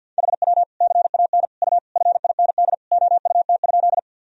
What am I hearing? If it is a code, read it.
Morse code, heard as a keyed tone, about 50 wpm, 700 Hz. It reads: HW CAN U FIND OUT?
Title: Sentences from Top 200 Words and Abbreviations 50wpm